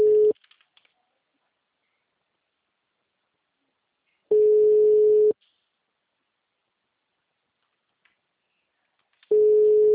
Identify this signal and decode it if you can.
Ringback tone (Europe, 425 Hz).
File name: out- -2031- - - .3052